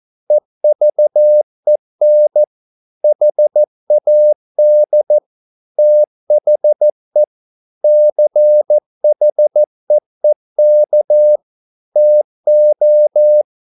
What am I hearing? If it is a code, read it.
Morse code, heard as a keyed tone, about 14 wpm, 600 Hz. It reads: EVEN HAD THE CHEEK TO